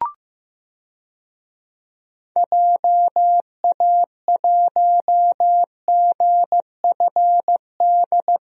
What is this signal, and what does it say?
Morse code, heard as a keyed tone, about 15 wpm, 700 Hz. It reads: JA1GFD
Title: DXCC-100 Call Signs - 6 Characters - Repeated 3x - 15wpm